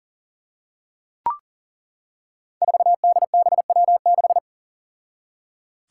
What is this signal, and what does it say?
Morse code, heard as a keyed tone, about 42 wpm, 690 Hz. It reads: 4DBW6
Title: Five Challenging Letter-Number Combinations 40wpm